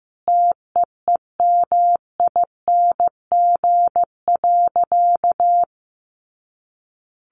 Morse code, heard as a keyed tone, about 15 wpm, 700 Hz. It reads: TEEMING.
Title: A Sound Of Thunder 15wpm